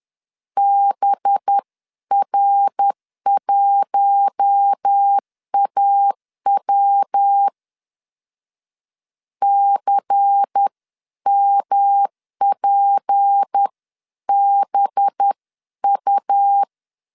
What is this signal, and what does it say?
Morse code, heard as a keyed tone, about 10 wpm, 800 Hz. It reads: BR1AW CMPBU
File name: LEZIONI / 8 YJ45 / 8 YJ 10wpm 2651